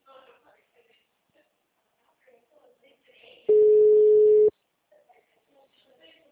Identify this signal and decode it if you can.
Ringback tone (Europe, 425 Hz), with speech.